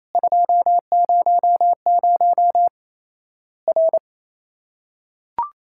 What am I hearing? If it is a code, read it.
Morse code, heard as a keyed tone, about 28 wpm, 700 Hz. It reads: 200 R
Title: Worked All Europe DX Contest - QTC Exchange - 28wpm